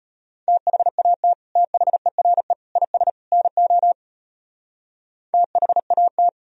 Morse code, heard as a keyed tone, about 31 wpm, 700 Hz. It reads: THAT THERE IS NO THAT
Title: Speed-Racing - Sets of 4 Words 25wpm